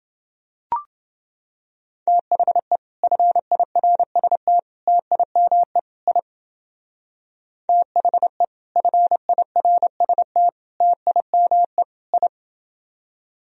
Morse code, heard as a keyed tone, about 28 wpm, 700 Hz. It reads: THE FIRST TIME I THE FIRST TIME I